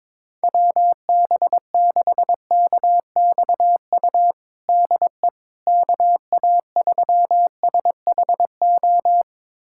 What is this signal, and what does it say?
Morse code, heard as a keyed tone, about 22 wpm, 700 Hz. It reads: WB6KXU DE KA3SHO
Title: USA Callsign DE Callsign 22wpm